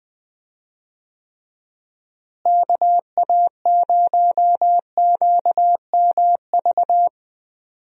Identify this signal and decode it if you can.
Morse code, heard as a keyed tone, about 21 wpm, 700 Hz. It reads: KA0QMV
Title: USA Call Signs - 6 Characters - Repeated 3x - 20wpm